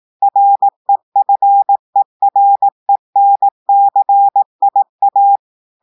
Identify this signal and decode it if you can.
Morse code, heard as a keyed tone, about 18 wpm, 800 Hz. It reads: REFERENCIA
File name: audios / portugais / portugais texte en clair 1 vitesse 18 mots minutes